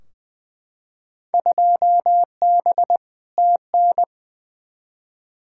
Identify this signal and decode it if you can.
Morse code, heard as a keyed tone, about 20 wpm, 700 Hz. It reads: 2B TN
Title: Field Day Exchanges Encore - 20wpm